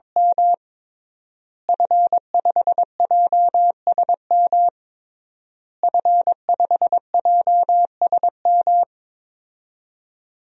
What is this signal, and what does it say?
Morse code, heard as a keyed tone, about 22 wpm, 700 Hz. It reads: M F5JSM F5JSM
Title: DXCC-100 Call Signs - 5 Characters - Repeated 3x - 22wpm